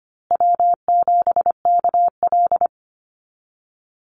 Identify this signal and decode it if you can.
Morse code, heard as a keyed tone, about 25 wpm, 700 Hz. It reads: W7KL
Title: USA Call Signs Repeated 2x 25wpm